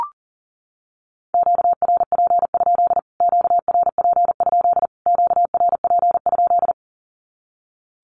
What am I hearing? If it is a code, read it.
Morse code, heard as a keyed tone, about 40 wpm, 700 Hz. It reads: QRP? QRP? QRP?
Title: Q-Codes Repeated 3x 40wpm